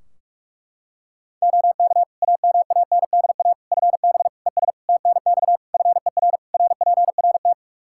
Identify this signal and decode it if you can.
Morse code, heard as a keyed tone, about 46 wpm, 700 Hz. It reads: OK AMANDA FB ES TNX FER RPRT
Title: K1USN SST QSO Practice 45wpm